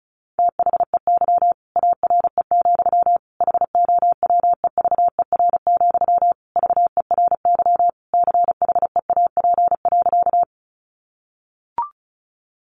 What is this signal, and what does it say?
Morse code, heard as a keyed tone, about 34 wpm, 700 Hz. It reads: THEY ARE, HOWEVER, VERY CHEAP.
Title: Sentences from Top 1200 Words 35wpm